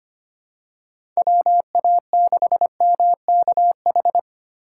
Morse code, heard as a keyed tone, about 25 wpm, 700 Hz. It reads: WA6MKH
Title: USA Call Signs - 4x Character Spacing - 25wpm